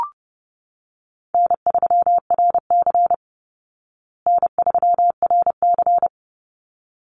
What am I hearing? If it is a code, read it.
Morse code, heard as a keyed tone, about 30 wpm, 700 Hz. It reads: N3RC N3RC
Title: Easy Call Signs Repeated 2x - 30wpm